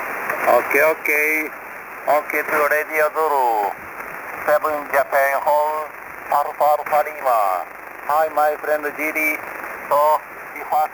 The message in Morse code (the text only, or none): none